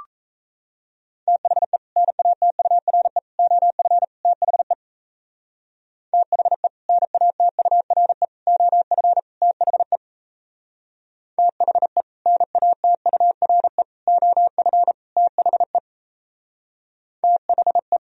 Morse code, heard as a keyed tone, about 34 wpm, 700 Hz. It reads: THE NATURE OF THE THE NATURE OF THE THE NATURE OF THE THE
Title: Speed-Racing - Sets of 4 Words 28wpm